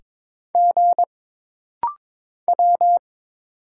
Morse code, heard as a keyed tone, about 22 wpm, 700 Hz. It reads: G W
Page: This rapid-fire timing is geared to push your ICR (Instant Character Recognition) skills.